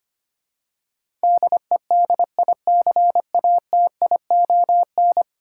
Morse code, heard as a keyed tone, about 25 wpm, 700 Hz. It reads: DEDICATION
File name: 10 Letter Words - First 3 Letters Given - 25wpm